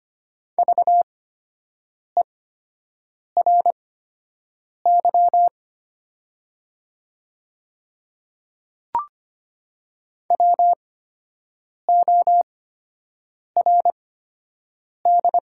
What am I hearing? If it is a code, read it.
Morse code, heard as a keyed tone, about 25 wpm, spaced as for 7 wpm, 700 Hz. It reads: VERY WORD
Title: 8x Character Spacing - Top 300 Words - 25wpm